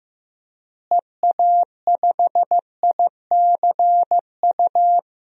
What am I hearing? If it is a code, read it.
Morse code, heard as a keyed tone, about 15 wpm, 700 Hz. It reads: EA5ICU
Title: DXCC-100 Call Signs - 6 Characters - Repeated 3x - 15wpm